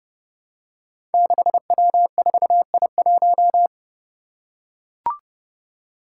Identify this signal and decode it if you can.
Morse code, heard as a keyed tone, about 30 wpm, 700 Hz. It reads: 6W4I1
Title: Five Challenging Letter-Number Combinations 30wpm